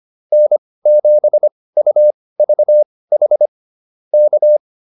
Morse code, heard as a keyed tone, about 25 wpm, spaced as for 18 wpm, 600 Hz. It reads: N7UVH K